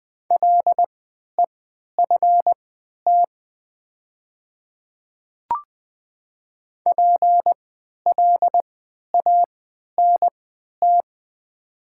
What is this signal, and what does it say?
Morse code, heard as a keyed tone, about 20 wpm, spaced as for 11 wpm, 700 Hz. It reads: LEFT PLANT
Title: 3x Character Spacing - Top 300 Words - 20wpm